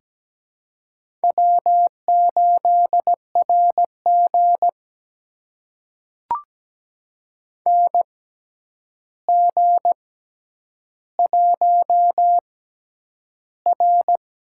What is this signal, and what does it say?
Morse code, heard as a keyed tone, about 17 wpm, spaced as for 9 wpm, 700 Hz. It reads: W8RG NG1R
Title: USA Call Signs - 4 Characters - 6x Character Spacing - 17wpm